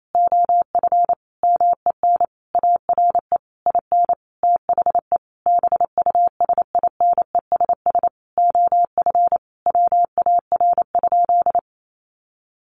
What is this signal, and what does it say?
Morse code, heard as a keyed tone, about 28 wpm, 700 Hz. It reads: OF MEN ARE IN THE BUSINESS OF WAR?
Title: Sentences from Top 1300 Words 28wpm